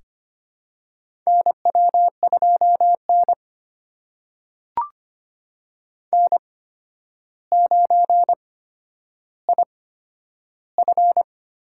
Morse code, heard as a keyed tone, about 25 wpm, spaced as for 11 wpm, 700 Hz. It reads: NW2N N9IF
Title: USA Call Signs - 8x Character Spacing - 25wpm